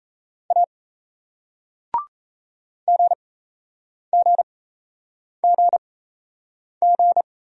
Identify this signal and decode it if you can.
Morse code, heard as a keyed tone, about 38 wpm, spaced as for 8 wpm, 700 Hz. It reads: A GGGG